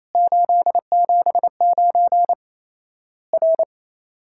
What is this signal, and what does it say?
Morse code, heard as a keyed tone, about 28 wpm, 700 Hz. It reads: 879 R
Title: Worked All Europe DX Contest - QTC Exchange - 28wpm